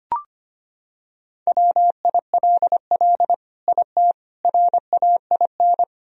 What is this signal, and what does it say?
Morse code, heard as a keyed tone, about 25 wpm, 700 Hz. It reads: WILL IT RAIN
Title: Sentences from Top 1000 Words - Review Part 1 - 25wpm